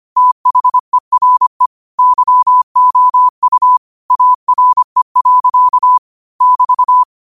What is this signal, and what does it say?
Morse code, heard as a keyed tone, about 25 wpm, 1000 Hz. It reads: THERE YOU ARE. <BT>
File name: 25wpm